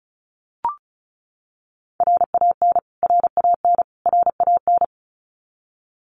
Morse code, heard as a keyed tone, about 34 wpm, 700 Hz. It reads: RAN RAN RAN
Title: Top 500 Words Repeated 35wpm